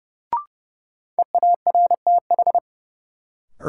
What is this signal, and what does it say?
Morse code, heard as a keyed tone, about 32 wpm, 700 Hz. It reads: EARTH
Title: Sentences from Top 200 Words - New Word Drill - 30wpm